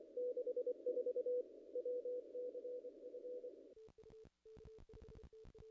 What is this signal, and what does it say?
Morse code, heard as a keyed tone, about 24 wpm, 500 Hz. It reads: 64 WCF
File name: contests / ss-cw-k5tr-op-n5rz -0000